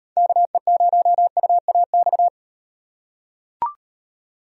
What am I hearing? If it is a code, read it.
Morse code, heard as a keyed tone, about 38 wpm, 700 Hz. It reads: KE0UAX